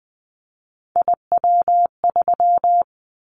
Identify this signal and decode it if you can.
Morse code, heard as a keyed tone, about 20 wpm, 700 Hz. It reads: IW3
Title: Callsign Prefixes for Top 50 DXCC - 20wpm